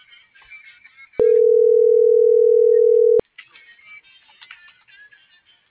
Ringback tone (North America), 440 + 480 Hz.